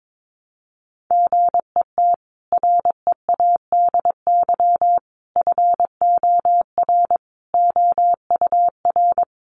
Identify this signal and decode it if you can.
Morse code, heard as a keyed tone, about 22 wpm, 700 Hz. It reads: GET READY FOR OUR
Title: Sentences from Top 1200 Words 22wpm